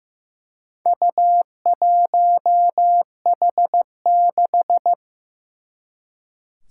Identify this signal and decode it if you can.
Morse code, heard as a keyed tone, about 15 wpm, 700 Hz. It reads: U1H6
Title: Four Challenging Letter-Number Combinations 15wpm